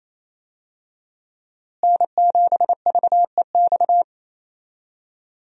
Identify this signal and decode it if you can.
Morse code, heard as a keyed tone, about 28 wpm, 700 Hz. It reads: N7VEX
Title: USA Call Signs - 5 Characters - Repeated 2x - 28wpm